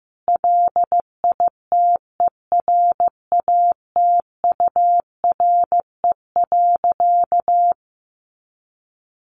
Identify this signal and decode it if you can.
Morse code, heard as a keyed tone, about 15 wpm, 700 Hz. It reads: LITERATURE.
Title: Sentences from Top 900 Words 15wpm